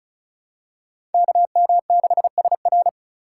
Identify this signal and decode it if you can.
Morse code, heard as a keyed tone, about 34 wpm, 700 Hz. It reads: KM6SR
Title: USA Call Signs - 5 Characters - 3x Character Spacing - 35wpm